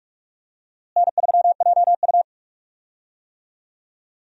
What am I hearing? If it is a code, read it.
Morse code, heard as a keyed tone, about 41 wpm, 700 Hz. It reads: N3JU
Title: USA Call Signs - 4 Characters - Repeated 3x - 45wpm